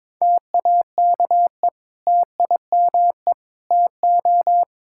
Morse code, heard as a keyed tone, about 21 wpm, 700 Hz. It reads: TAKE TIME TO
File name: practice / 22wpm / Sentences with 4-Word Phrases - 22wpm